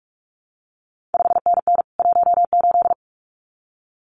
Morse code, heard as a keyed tone, about 46 wpm, 700 Hz. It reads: 5NN 18
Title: CQ WW DX Contest Exchange Encore - 45wpm